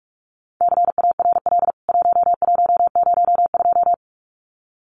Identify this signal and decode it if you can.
Morse code, heard as a keyed tone, about 44 wpm, 700 Hz. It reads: CARL 1102
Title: CWT Exchange 45wpm